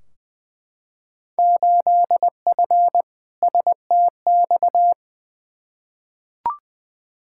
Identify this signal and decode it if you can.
Morse code, heard as a keyed tone, about 20 wpm, 700 Hz. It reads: 8F STX
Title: Field Day Exchanges Encore - 20wpm